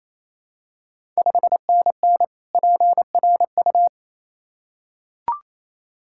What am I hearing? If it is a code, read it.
Morse code, heard as a keyed tone, about 28 wpm, 700 Hz. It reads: 5NN PRU
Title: Ontario QSO Party Exchange 28wpm